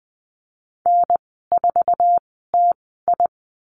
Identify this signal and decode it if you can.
Morse code, heard as a keyed tone, about 20 wpm, spaced as for 15 wpm, 700 Hz. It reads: N4TI
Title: 048 Lesson 9g - Number 4 - Call Signs 20wpm